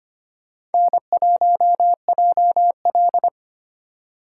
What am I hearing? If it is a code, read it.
Morse code, heard as a keyed tone, about 32 wpm, 700 Hz. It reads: N1JL